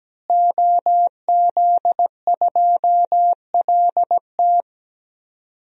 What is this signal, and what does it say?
Morse code, heard as a keyed tone, about 17 wpm, 700 Hz. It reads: OZ2LT
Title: DXCC-100 Call Signs - 5 Characters - Repeated 2x - 17wpm